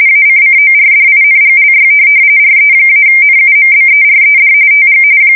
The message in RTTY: ACCORDINGLY RE-NUMBERED TG

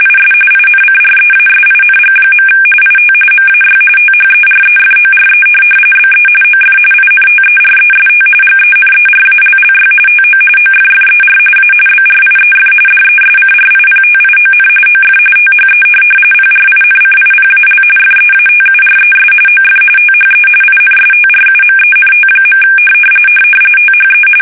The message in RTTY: IS INCREASING. IT MAY THEREFORE BE OF INTEREST GENERALLY TO OFFER A BRIEF DESCRIPTION OF THE PRINCIPAL FEATURES OF THE INSTRUMENT. THE

